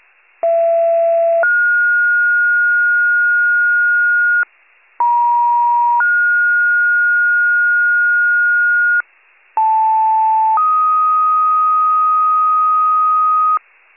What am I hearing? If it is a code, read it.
Two-tone page, heard as a one-second tone, then a three-second tone, about 670 then 1430 Hz, about 950 then 1430 Hz, about 880 then 1250 Hz, with noise. It is heard from 0.4 to 13.6 s.